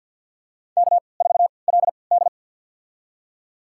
Morse code, heard as a keyed tone, about 52 wpm, spaced as for 29 wpm, 700 Hz. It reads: K4LD
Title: USA Call Signs - 4 Characters - 3x Character Spacing - 50wpm